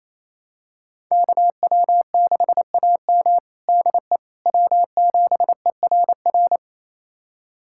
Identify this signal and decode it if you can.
Morse code, heard as a keyed tone, about 28 wpm, 700 Hz. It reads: KW6AM DE W7ERR
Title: USA Callsign DE Callsign - 5 Characters - 28wpm